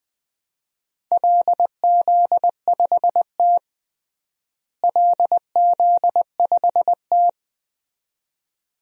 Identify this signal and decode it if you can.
Morse code, heard as a keyed tone, about 20 wpm, 700 Hz. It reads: LZ5T LZ5T
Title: DXCC-100 Call Signs - 4 Characters - Repeated 2x - 20wpm